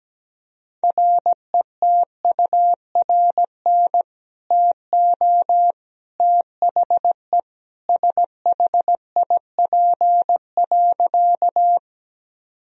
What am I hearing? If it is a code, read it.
Morse code, heard as a keyed tone, about 17 wpm, 700 Hz. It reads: RETURN TO THE SHIP.
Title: Sentences from Top 1000 Words - Review Part 2 - 17wpm